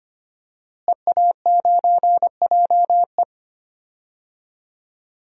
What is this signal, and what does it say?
Morse code, heard as a keyed tone, about 25 wpm, 700 Hz. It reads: EA9JE
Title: DXCC-100 Call Signs - 5 Characters - Repeated 2x - 25wpm